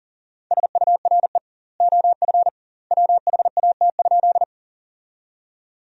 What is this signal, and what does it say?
Morse code, heard as a keyed tone, about 40 wpm, 700 Hz. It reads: SURE OF WHAT?
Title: 2x Word-Spacing - Sentences from Top 300 Words - 40wpm